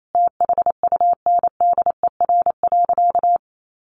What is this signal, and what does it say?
Morse code, heard as a keyed tone, about 28 wpm, 700 Hz. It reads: THUNDER.
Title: A Sound Of Thunder 28wpm